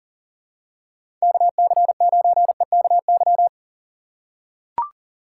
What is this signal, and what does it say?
Morse code, heard as a keyed tone, about 39 wpm, 700 Hz. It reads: KC9EKY
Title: USA Call Signs - 6 Characters - 4x Character Spacing - 40wpm